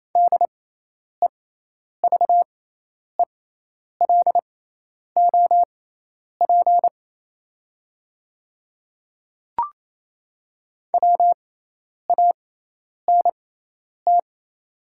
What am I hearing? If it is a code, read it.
Morse code, heard as a keyed tone, about 27 wpm, spaced as for 10 wpm, 690 Hz. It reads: DEVELOP WANT
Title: 6x Character Spacing - Top 500 Words - 28wpm